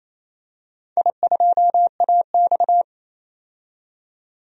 Morse code, heard as a keyed tone, about 29 wpm, 700 Hz. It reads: I2AX
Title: DXCC-100 Call Signs - 4 Characters - 28wpm